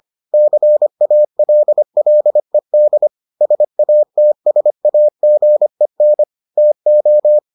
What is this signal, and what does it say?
Morse code, heard as a keyed tone, about 25 wpm, 600 Hz. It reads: CALLED SATSAGEN TO